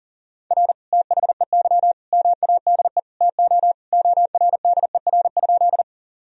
Morse code, heard as a keyed tone, about 37 wpm, 700 Hz. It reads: R THEY MADE TO ORDER?
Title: Sentences from Top 400 Words and Abbreviations 40wpm